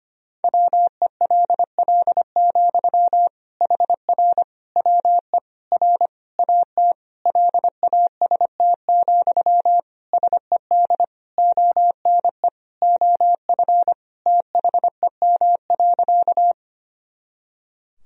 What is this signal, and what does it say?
Morse code, heard as a keyed tone, about 25 wpm, 700 Hz. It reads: WELL, HR WE R AT LAST, SED ONE OF THEM.
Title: Sentences from Top 300 Words and Abbreviations 25wpm